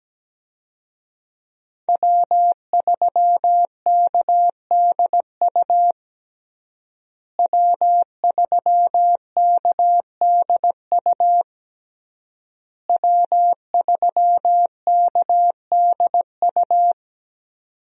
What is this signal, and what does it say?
Morse code, heard as a keyed tone, about 17 wpm, 700 Hz. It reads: W3KDU W3KDU W3KDU
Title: USA Call Signs - 5 Characters - Repeated 3x - 17wpm